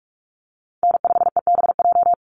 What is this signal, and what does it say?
Morse code, heard as a keyed tone, about 44 wpm, 700 Hz. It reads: N5EBJ
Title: USA Call Signs 45wpm